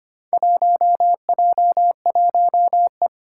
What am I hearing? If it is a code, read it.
Morse code, heard as a keyed tone, about 25 wpm, 700 Hz. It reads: 1J1E
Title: Four Challenging Letter-Number Combinations 25wpm